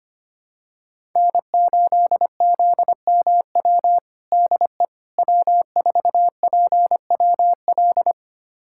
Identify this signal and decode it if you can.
Morse code, heard as a keyed tone, about 25 wpm, 700 Hz. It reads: N8ZMW DE W4PWL
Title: USA Callsign DE Callsign - 5 Characters - 25wpm